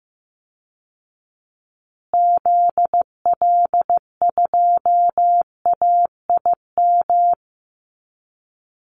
Morse code, heard as a keyed tone, about 15 wpm, 700 Hz. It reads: ZL2AIM
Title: Call Signs Repeated 3x - 15wpm